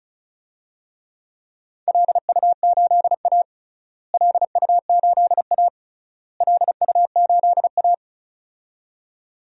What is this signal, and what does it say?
Morse code, heard as a keyed tone, about 34 wpm, 700 Hz. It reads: LU8A LU8A LU8A
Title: DXCC-100 Call Signs - 4 Characters - Repeated 3x - 35wpm